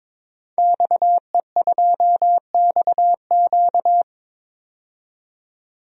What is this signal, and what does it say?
Morse code, heard as a keyed tone, about 22 wpm, 700 Hz. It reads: XE2XQ
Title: DXCC-100 Call Signs - 5 Characters - 22wpm